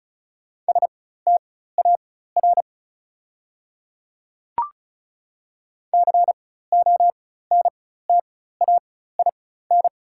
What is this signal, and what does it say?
Morse code, heard as a keyed tone, about 36 wpm, spaced as for 16 wpm, 690 Hz. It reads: STAR CONTAIN
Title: 4x Character Spacing - Top 500 Words - 35wpm